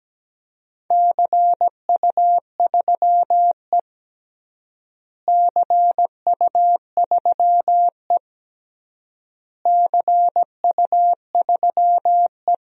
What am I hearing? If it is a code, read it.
Morse code, heard as a keyed tone, about 17 wpm, 700 Hz. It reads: CU3E CU3E CU3E
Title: DXCC-100 Call Signs - 4 Characters - Repeated 3x - 17wpm